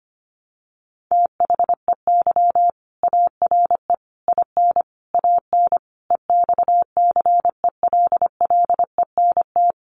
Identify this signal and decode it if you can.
Morse code, heard as a keyed tone, about 25 wpm, 700 Hz. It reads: THEY ARE IN AN EXCELLENT
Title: Sentences from Top 1000 Words 25wpm